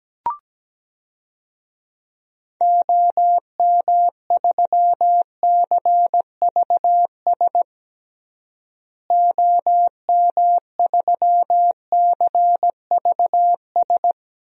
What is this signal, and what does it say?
Morse code, heard as a keyed tone, about 17 wpm, 700 Hz. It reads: OM3CVS OM3CVS
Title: DXCC-100 Call Signs - 6 Characters - Repeated 3x - 17wpm